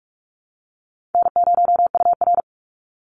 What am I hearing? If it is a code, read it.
Morse code, heard as a keyed tone, about 44 wpm, 700 Hz. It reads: N0UR
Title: Easy Call Signs - 45wpm